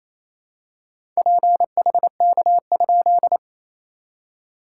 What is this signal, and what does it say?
Morse code, heard as a keyed tone, about 28 wpm, 690 Hz. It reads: PHK?
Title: SOTA QSO Practice 28wpm